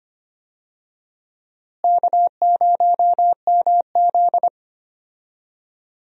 Morse code, heard as a keyed tone, about 25 wpm, 700 Hz. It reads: K0MZ
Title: USA Call Signs - 4 Characters - Repeated 3x - 25wpm